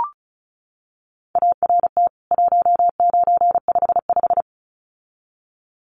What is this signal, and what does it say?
Morse code, heard as a keyed tone, about 34 wpm, 700 Hz. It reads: ART 1955
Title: CWT Exchange 35wpm